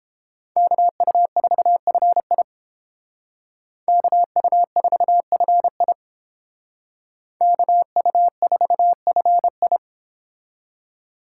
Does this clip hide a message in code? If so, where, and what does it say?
Yes: Morse code from 0.6 to 9.8 s, heard as a keyed tone, about 29 wpm, 700 Hz. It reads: KU4FI KU4FI KU4FI